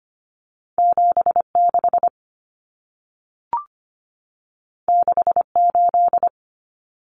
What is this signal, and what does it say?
Morse code, heard as a keyed tone, about 25 wpm, 700 Hz. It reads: 76 68
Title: Numbers - 2-Digit 25wpm